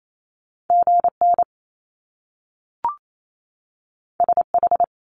Morse code, heard as a keyed tone, about 28 wpm, 700 Hz. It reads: GN SH
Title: Top Two Letter Combinations in Words 28wpm